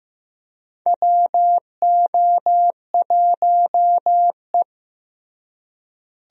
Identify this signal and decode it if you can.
Morse code, heard as a keyed tone, about 15 wpm, 700 Hz. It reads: WO1E
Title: USA Call Signs - 4 Characters - Repeated 2x - 15wpm